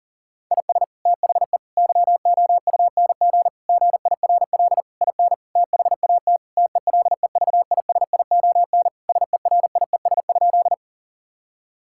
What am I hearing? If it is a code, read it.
Morse code, heard as a keyed tone, about 40 wpm, 700 Hz. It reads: IS THE YOUNG GIRL IN THAT TELEVISION SERIES?
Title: Sentences from Top 1000 Words - Review Part 5 - 40wpm